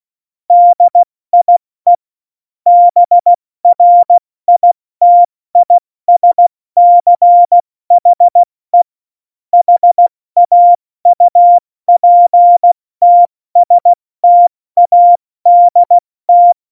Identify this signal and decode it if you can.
Morse code, heard as a keyed tone, about 16 wpm, 700 Hz. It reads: DIE BRITISCHE HAUPTSTADT
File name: Morsekurs / Witze / Witz 19 Mai, 2020 70 F80